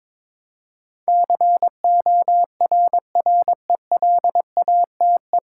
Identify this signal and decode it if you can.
Morse code, heard as a keyed tone, about 22 wpm, 700 Hz. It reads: CORRELATE
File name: practice / 22wpm / Nine Letter Words - 22wpm